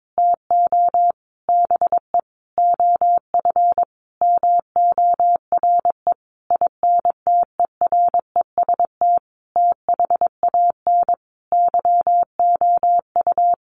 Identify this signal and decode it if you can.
Morse code, heard as a keyed tone, about 22 wpm, 700 Hz. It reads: TO BE OF MORE INTEREST THAN YOU
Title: Sentences from Top 400 Words 22wpm